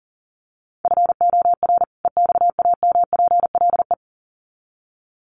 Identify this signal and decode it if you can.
Morse code, heard as a keyed tone, about 37 wpm, 700 Hz. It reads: FOR EXAMPLE
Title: Sets of 2 Words 40wpm